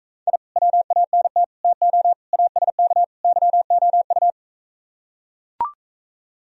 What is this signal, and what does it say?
Morse code, heard as a keyed tone, about 43 wpm, 700 Hz. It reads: I WANT TO ASK YOU